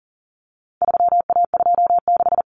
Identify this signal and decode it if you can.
Morse code, heard as a keyed tone, about 40 wpm, 700 Hz. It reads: 3A26
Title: Common Cut Numbers - 4-Digit 40wpm